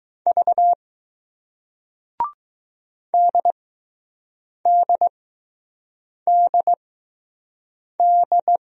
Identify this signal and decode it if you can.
Morse code, heard as a keyed tone, about 20 wpm, spaced as for 6 wpm, 700 Hz. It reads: V DDDD